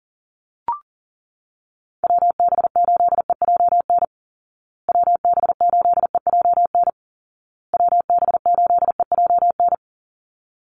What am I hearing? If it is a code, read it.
Morse code, heard as a keyed tone, about 40 wpm, 700 Hz. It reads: WB8EJN WB8EJN WB8EJN
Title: Call Signs Repeated 2x - 40wpm